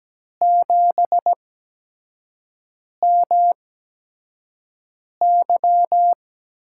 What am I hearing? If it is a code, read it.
Morse code, heard as a keyed tone, about 17 wpm, spaced as for 5 wpm, 700 Hz. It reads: 7MY
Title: USA Call Signs - 8x Character Spacing - 17wpm